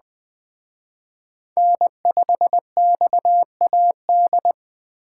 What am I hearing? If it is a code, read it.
Morse code, heard as a keyed tone, about 20 wpm, 700 Hz. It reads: N5XAD
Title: Speed Racing - USA 1x3 Call Signs 17wpm